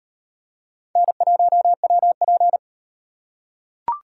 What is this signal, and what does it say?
Morse code, heard as a keyed tone, about 38 wpm, 700 Hz. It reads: N1WP